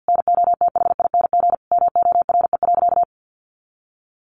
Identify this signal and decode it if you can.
Morse code, heard as a keyed tone, about 50 wpm, 700 Hz. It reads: NOTHING MORE.
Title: Sentences from Top 800 Words 50wpm